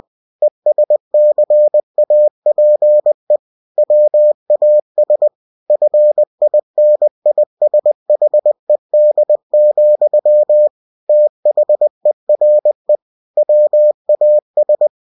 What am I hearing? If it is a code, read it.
Morse code, heard as a keyed tone, about 20 wpm, 600 Hz. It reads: ESCAPE WAS FINISHED, THERE WAS
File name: cw-books / winnie-the-pooh / winnie-20wpm-page-0094